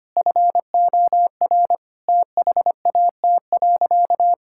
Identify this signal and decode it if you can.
Morse code, heard as a keyed tone, about 25 wpm, 700 Hz. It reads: FOR THAT.
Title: Sentences from Top 1000 Words - Review Part 5 - 25wpm